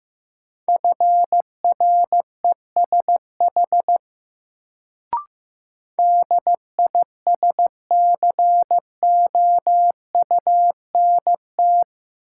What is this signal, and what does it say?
Morse code, heard as a keyed tone, about 15 wpm, 700 Hz. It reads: FRESH DISCOUNT
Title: Sentences from Top 1200 Words - New Word Drill - 15wpm